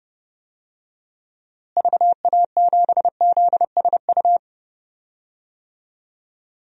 Morse code, heard as a keyed tone, about 29 wpm, 700 Hz. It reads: VA7ZSU
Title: DXCC-100 Call Signs - 6 Characters - Repeated 2x - 30wpm